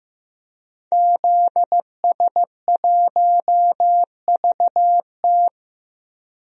Morse code, heard as a keyed tone, about 15 wpm, 700 Hz. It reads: ZS1VT
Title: DXCC-100 Call Signs - 5 Characters - 15wpm